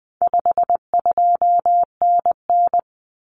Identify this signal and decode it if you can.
Morse code, heard as a keyed tone, about 20 wpm, 700 Hz. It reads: 52NN
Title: Common Cut Numbers - 4-Digit 20wpm